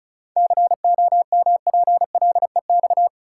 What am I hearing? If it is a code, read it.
Morse code, heard as a keyed tone, about 34 wpm, 700 Hz. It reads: COMPLEX
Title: Sentences from Top 900 Words - New Word Drill - 35wpm